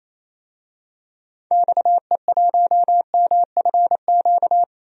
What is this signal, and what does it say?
Morse code, heard as a keyed tone, about 28 wpm, 700 Hz. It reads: XE1MFQ
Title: DXCC-100 Call Signs - 6 Characters - Repeated 2x - 28wpm